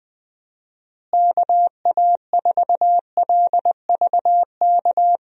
Morse code, heard as a keyed tone, about 20 wpm, 700 Hz. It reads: KA4LVK
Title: USA Call Signs - 6 Characters - Repeated 3x - 20wpm